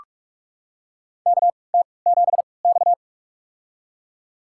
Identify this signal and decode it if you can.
Morse code, heard as a keyed tone, about 48 wpm, spaced as for 26 wpm, 690 Hz. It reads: KT7X
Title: USA Call Signs - 4 Characters - 3x Character Spacing - 45wpm